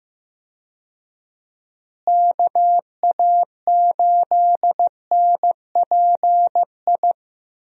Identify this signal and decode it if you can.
Morse code, heard as a keyed tone, about 15 wpm, 700 Hz. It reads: KA8NPI
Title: USA Call Signs - 6 Characters - Repeated 2x - 15wpm